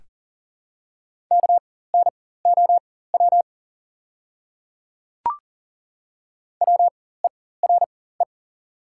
Morse code, heard as a keyed tone, about 40 wpm, spaced as for 19 wpm, 700 Hz. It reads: KNOW WERE
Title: 4x Character Spacing - Top 500 Words - 40wpm